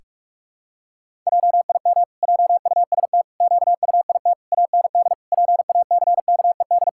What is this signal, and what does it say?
Morse code, heard as a keyed tone, about 41 wpm, 700 Hz. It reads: JIM JUST QUIT AND PACKED
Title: Pangram Challenge v2 - 45wpm